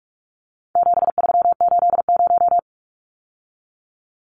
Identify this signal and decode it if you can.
Morse code, heard as a keyed tone, about 44 wpm, 700 Hz. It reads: 7380
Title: Numbers - 4-Digit 45wpm